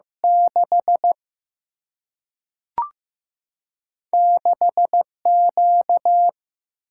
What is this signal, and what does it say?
Morse code, heard as a keyed tone, about 15 wpm, 700 Hz. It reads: 6 6 Q
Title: Single-Letter Single-Number 15wpm